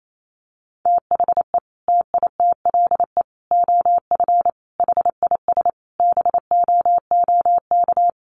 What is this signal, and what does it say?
Morse code, heard as a keyed tone, about 28 wpm, 700 Hz. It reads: THE TITLE OF HIS BOOK
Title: Sentences from Top 1200 Words 28wpm